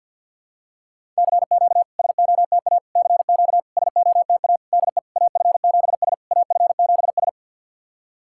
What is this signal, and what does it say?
Morse code, heard as a keyed tone, about 50 wpm, 700 Hz. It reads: CQ SOTA CQ SOTA DE AF7S AF7S
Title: SOTA QSO Practice 50wpm